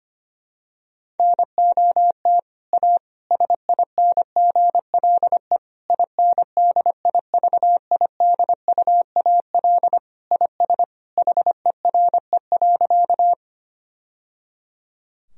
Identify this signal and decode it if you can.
Morse code, heard as a keyed tone, about 25 wpm, 700 Hz. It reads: NOT A SINGLE INDIVIDUAL IS HERE.
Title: Sentences from Top 1000 Words - Review Part 4 - 25wpm